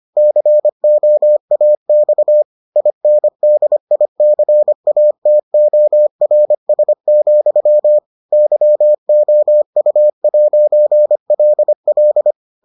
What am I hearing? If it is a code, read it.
Morse code, heard as a keyed tone, about 25 wpm, 600 Hz. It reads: COAX INDICATORS, YOU'LL